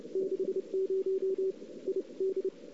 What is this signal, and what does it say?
Morse code, heard as a keyed tone, about 30 wpm, spaced as for 26 wpm, 410 Hz. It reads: 60 ID
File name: CW / 2008-ss-cw-k5tr-op-n5rz -02:00